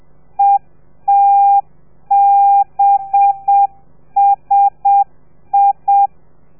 Morse code, heard as a keyed tone, about 7 wpm, 800 Hz. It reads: ETBSI
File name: telegrafia / i2tfj / Lez4 30